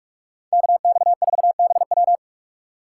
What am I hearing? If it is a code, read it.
Morse code, heard as a keyed tone, about 45 wpm, 700 Hz. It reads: KX4BW